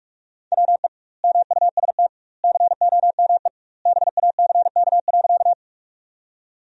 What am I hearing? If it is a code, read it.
Morse code, heard as a keyed tone, about 46 wpm, 700 Hz. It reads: WE MUST COME BACK.
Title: 2x Word-Spacing - Sentences from Top 200 Words - 45wpm